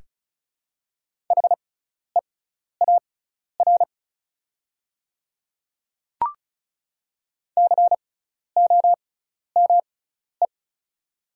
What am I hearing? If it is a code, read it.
Morse code, heard as a keyed tone, about 36 wpm, spaced as for 12 wpm, 700 Hz. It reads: HEAR COME
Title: 6x Character Spacing - Top 300 Words - 35wpm